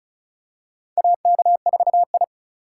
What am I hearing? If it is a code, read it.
Morse code, heard as a keyed tone, about 36 wpm, 700 Hz. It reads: AK4I
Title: USA Call Signs - 4 Characters - 4x Character Spacing - 35wpm